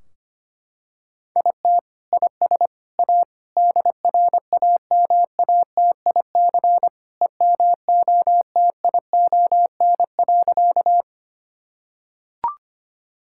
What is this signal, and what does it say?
Morse code, heard as a keyed tone, about 25 wpm, 700 Hz. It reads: IT IS A DRAMATIC EMOTION.
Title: Sentences from Top 1600 Words 25wpm